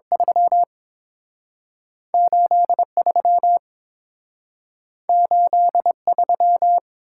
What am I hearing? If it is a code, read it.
Morse code, heard as a keyed tone, about 30 wpm, 700 Hz. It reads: 3 8 3 8 3